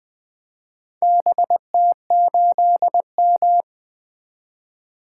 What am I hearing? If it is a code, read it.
Morse code, heard as a keyed tone, about 20 wpm, 700 Hz. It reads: BT8M
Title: Four Letter-Number Combinations - 20wpm